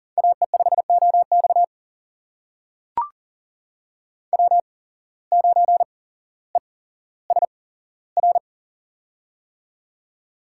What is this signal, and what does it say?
Morse code, heard as a keyed tone, about 40 wpm, spaced as for 17 wpm, 700 Hz. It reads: AE5OX W9ESR
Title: USA Call Signs - 5 Characters - 8x Character Spacing - 40wpm